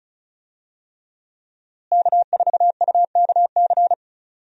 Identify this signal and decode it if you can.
Morse code, heard as a keyed tone, about 36 wpm, 700 Hz. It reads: K4UKC
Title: USA Call Signs - 5 Characters - Repeated 3x - 35wpm